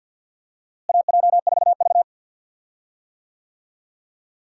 Morse code, heard as a keyed tone, about 51 wpm, 700 Hz. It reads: AJ4V
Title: USA Call Signs - 4 Characters - Repeated 3x - 50wpm